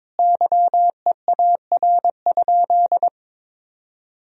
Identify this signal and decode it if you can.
Morse code, heard as a keyed tone, about 22 wpm, 700 Hz. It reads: YEAR?
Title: Sentences from Top 1000 Words - Review Part 3 - 22wpm